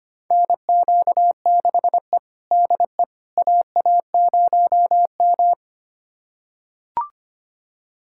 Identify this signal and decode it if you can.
Morse code, heard as a keyed tone, about 25 wpm, 700 Hz. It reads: NQ6E DE AA0M
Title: USA Callsign DE Callsign - 4 Characters - 25wpm